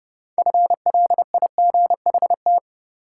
Morse code, heard as a keyed tone, about 29 wpm, 700 Hz. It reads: FLIGHT
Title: Six Letter Words - 30wpm